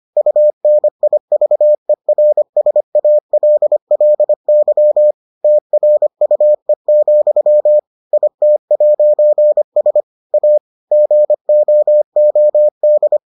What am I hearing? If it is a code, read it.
Morse code, heard as a keyed tone, about 25 wpm, 600 Hz. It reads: UNIVERSALLY TRUE, IT'S A GOOD